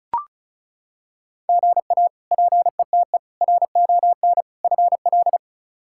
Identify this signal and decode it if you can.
Morse code, heard as a keyed tone, about 34 wpm, 700 Hz. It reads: GA PETE RON FL
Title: K1USN SST QSO Practice - 35wpm